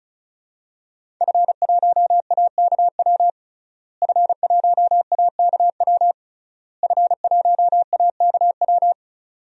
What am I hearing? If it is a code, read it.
Morse code, heard as a keyed tone, about 34 wpm, 700 Hz. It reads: F1AKW F1AKW F1AKW
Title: DXCC-100 Call Signs - 5 Characters - Repeated 3x - 35wpm